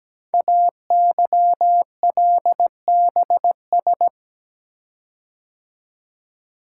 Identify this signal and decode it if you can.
Morse code, heard as a keyed tone, about 17 wpm, 700 Hz. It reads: AYLBS
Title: Code Groups - 5 Letter:Numbers - 17wpm